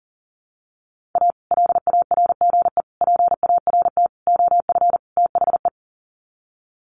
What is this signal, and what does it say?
Morse code, heard as a keyed tone, about 40 wpm, 700 Hz. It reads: A LARGE PART OF THE
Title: Sets of 5 Words Encore2 40wpm